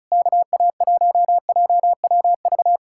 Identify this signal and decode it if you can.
Morse code, heard as a keyed tone, about 38 wpm, 700 Hz. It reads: KA1JWV